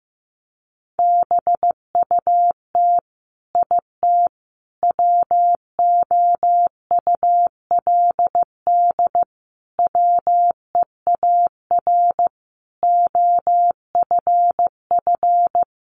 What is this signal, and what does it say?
Morse code, heard as a keyed tone, about 15 wpm, 700 Hz. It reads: BUT IT WOULD WEAR OFF
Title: Sentences from Top 1500 Words 15wpm